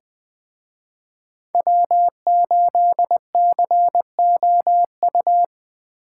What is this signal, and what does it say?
Morse code, heard as a keyed tone, about 20 wpm, 700 Hz. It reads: W8COU
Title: Speed Racing - USA 1x3 Call Signs 17wpm